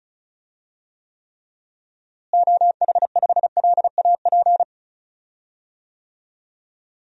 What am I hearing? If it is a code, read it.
Morse code, heard as a keyed tone, about 34 wpm, 700 Hz. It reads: OH5LAP
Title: DXCC-100 Call Signs - 6 Characters - Repeated 3x - 35wpm